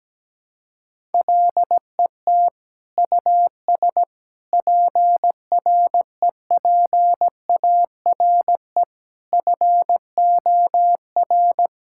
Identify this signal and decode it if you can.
Morse code, heard as a keyed tone, about 17 wpm, 700 Hz. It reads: LET US PREPARE FOR
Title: Sentences from Top 1000 Words - Review Part 6 - 17wpm